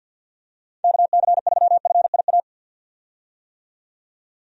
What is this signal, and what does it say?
Morse code, heard as a keyed tone, about 50 wpm, 700 Hz. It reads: KC3FIA
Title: USA Call Signs - 6 Characters - 50wpm